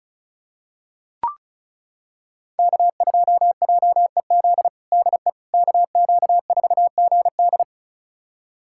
Morse code, heard as a keyed tone, about 36 wpm, 700 Hz. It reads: K2JEZ DE KQ4GD
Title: USA Callsign DE Callsign - 5 Characters - 35wpm